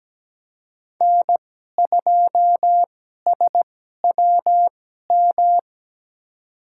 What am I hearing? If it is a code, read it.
Morse code, heard as a keyed tone, about 17 wpm, spaced as for 12 wpm, 700 Hz. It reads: N2SWM
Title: USA Call Signs - 5 Characters - 2x Character Spacing - 17wpm